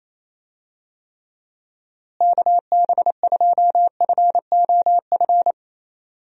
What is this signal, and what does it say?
Morse code, heard as a keyed tone, about 28 wpm, 700 Hz. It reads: KB2FOF
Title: USA Call Signs - 6 Characters - Repeated 2x - 28wpm